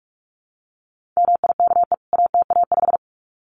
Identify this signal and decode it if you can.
Morse code, heard as a keyed tone, about 44 wpm, 700 Hz. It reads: MIKE ATA5
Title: CWT Exchange 45wpm